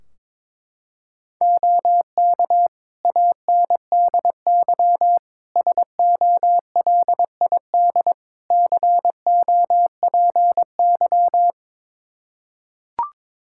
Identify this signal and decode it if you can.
Morse code, heard as a keyed tone, about 22 wpm, 700 Hz. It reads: OK ANDY SOLID COPY
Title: K1USN SST QSO Practice 22wpm